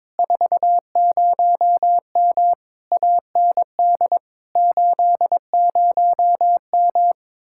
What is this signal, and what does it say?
Morse code, heard as a keyed tone, about 22 wpm, 700 Hz. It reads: 40M AND 80M
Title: Speed-Racing Common QSO Phrases Encore - 22wpm